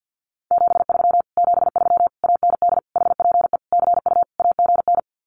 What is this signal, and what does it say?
Morse code, heard as a keyed tone, about 50 wpm, 700 Hz. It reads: 73 73 AND HPE CU AGN
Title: Common QSO Phrases Encore 50wpm